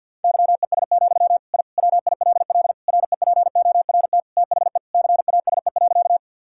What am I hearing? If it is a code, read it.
Morse code, heard as a keyed tone, about 52 wpm, 700 Hz. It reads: YES, I WILL REPORT THE CASE.
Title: Sentences from Top 1000 Words - Review Part 6 - 50wpm